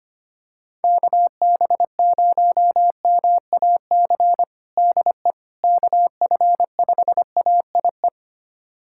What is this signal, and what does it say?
Morse code, heard as a keyed tone, about 25 wpm, 700 Hz. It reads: KB0MAC DE KF5AIE
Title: USA Callsign DE Callsign - 6 Characters - 25wpm